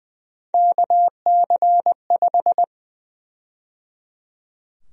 Morse code, heard as a keyed tone, about 20 wpm, 700 Hz. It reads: KC5
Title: USA Call Sign Prefixes - 20wpm